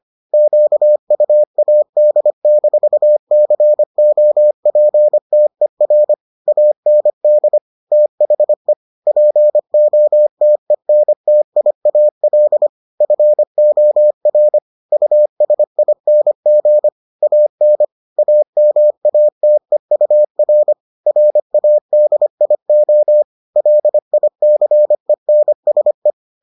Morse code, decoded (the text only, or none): QUAD-COPTER AND THE POTENTIAL FOR USING AN AMATEUR RADIO LICENSE